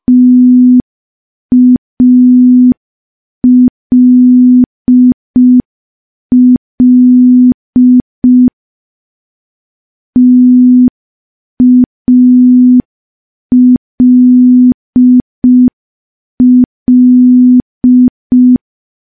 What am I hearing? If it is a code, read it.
Morse code, heard as a keyed tone, about 5 wpm, 250 Hz. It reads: TALL TALL
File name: morse / morse-training-05WPM